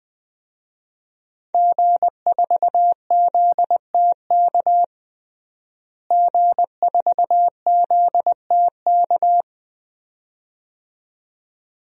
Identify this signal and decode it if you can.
Morse code, heard as a keyed tone, about 21 wpm, 700 Hz. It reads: G4ZTK G4ZTK
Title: DXCC-100 Call Signs - 5 Characters - Repeated 2x - 20wpm